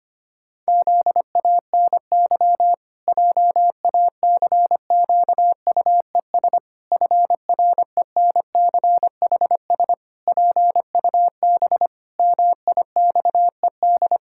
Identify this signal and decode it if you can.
Morse code, heard as a keyed tone, about 25 wpm, 700 Hz. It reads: ZANY JACQUES FRENCHS PUB MIXED
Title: Pangram Challenge v2 - 25wpm